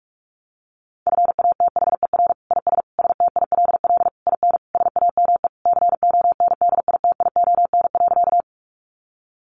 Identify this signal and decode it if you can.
Morse code, heard as a keyed tone, about 44 wpm, 700 Hz. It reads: FATHER IS STILL IN SAME CONDITION.
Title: Sentences from Top 1000 Words 45wpm